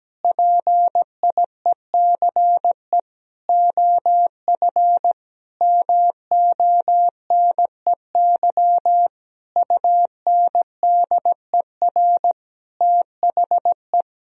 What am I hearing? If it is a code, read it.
Morse code, heard as a keyed tone, about 17 wpm, 700 Hz. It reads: PIECE OF MONEY UNDER THE
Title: Sentences from Top 1000 Words - Review Part 2 - 17wpm